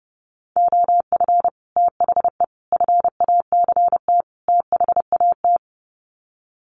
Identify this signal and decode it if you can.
Morse code, heard as a keyed tone, about 30 wpm, 700 Hz. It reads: OF THE FACT THAT
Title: Sets of 4 Words 30wpm